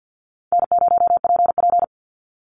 Morse code, heard as a keyed tone, about 50 wpm, 700 Hz. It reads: N0PP
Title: Easy Call Signs Repeated 3x - 50wpm